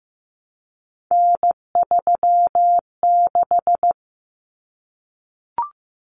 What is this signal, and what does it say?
Morse code, heard as a keyed tone, about 15 wpm, 700 Hz. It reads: N36
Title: Common Cut Numbers - 3-Digit 15wpm